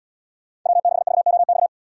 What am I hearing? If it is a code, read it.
Morse code, heard as a keyed tone, about 77 wpm, 700 Hz. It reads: W6VOX